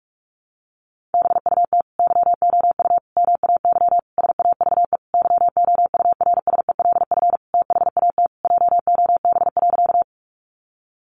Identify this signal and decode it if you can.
Morse code, heard as a keyed tone, about 44 wpm, 700 Hz. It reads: BUT YOU MAY SAVE YOURSELF THAT JOB.
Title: Sentences from Top 1100 Words 45wpm